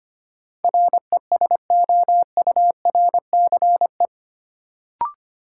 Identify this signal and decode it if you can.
Morse code, heard as a keyed tone, about 25 wpm, 700 Hz. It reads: RESOURCE
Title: Sentences from Top 1300 Words - New Word Drill Alt - 25wpm